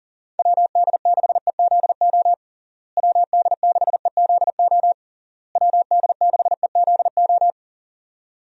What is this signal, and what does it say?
Morse code, heard as a keyed tone, about 37 wpm, 700 Hz. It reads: WD6EZO WD6EZO WD6EZO
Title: USA Call Signs - 6 Characters - Repeated 3x - 40wpm